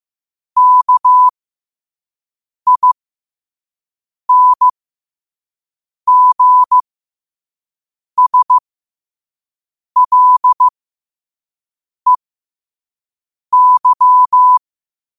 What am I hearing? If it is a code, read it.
Morse code, heard as a keyed tone, about 15 wpm, spaced as for 5 wpm, 1000 Hz. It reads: KINGSLEY
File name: qotdfiles / 5wpm